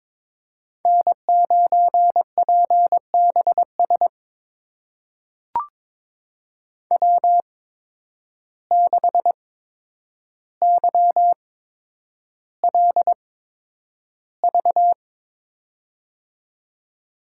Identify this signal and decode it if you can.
Morse code, heard as a keyed tone, about 22 wpm, spaced as for 9 wpm, 700 Hz. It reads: N9PBS W6YLV
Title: USA Call Signs - 5 Characters - 8x Character Spacing - 22wpm